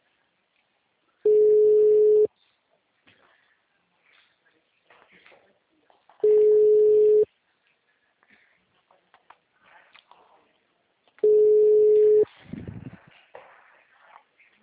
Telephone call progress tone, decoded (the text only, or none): ringback tone